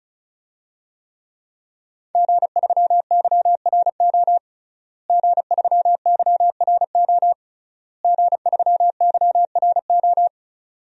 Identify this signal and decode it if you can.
Morse code, heard as a keyed tone, about 36 wpm, 700 Hz. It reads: G3YRO G3YRO G3YRO
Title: DXCC-100 Call Signs - 5 Characters - Repeated 3x - 35wpm